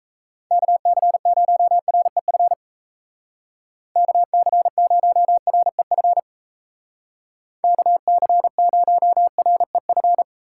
Morse code, heard as a keyed tone, about 34 wpm, 700 Hz. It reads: KC0REF KC0REF KC0REF
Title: Speed Racing - USA 2x3 Call Signs 28wpm